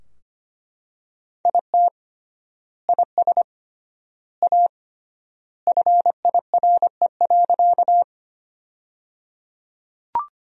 Morse code, heard as a keyed tone, about 25 wpm, 700 Hz. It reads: IT IS A FIRE.
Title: 3x Word-Spacing - Sentences from Top 400 Words - 25wpm